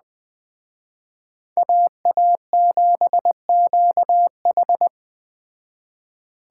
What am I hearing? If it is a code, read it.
Morse code, heard as a keyed tone, about 20 wpm, 700 Hz. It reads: AA7QH